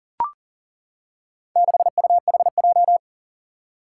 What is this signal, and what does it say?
Morse code, heard as a keyed tone, about 42 wpm, 700 Hz. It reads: 6UHJ
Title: Four Challenging Letter-Number Combinations 40wpm